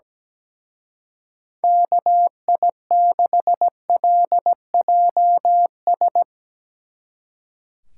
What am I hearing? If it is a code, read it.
Morse code, heard as a keyed tone, about 18 wpm, 700 Hz. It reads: KI6LJS
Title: Speed Racing - USA 2x3 Call Signs 17wpm